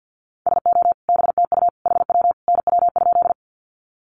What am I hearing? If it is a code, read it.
Morse code, heard as a keyed tone, about 50 wpm, 700 Hz. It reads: SO BTU HW NW?
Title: Common QSO Phrases 50wpm